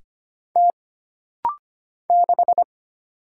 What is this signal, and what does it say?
Morse code, heard as a keyed tone, about 25 wpm, 700 Hz. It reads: T 6
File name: Single Character (Letter:Number:Punctuation:Prosign) - Rapid-Fire - 25wpm